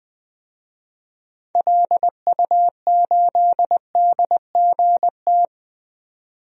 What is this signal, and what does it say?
Morse code, heard as a keyed tone, about 20 wpm, 700 Hz. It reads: LU8DGT
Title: DXCC-100 Call Signs - 6 Characters - 20wpm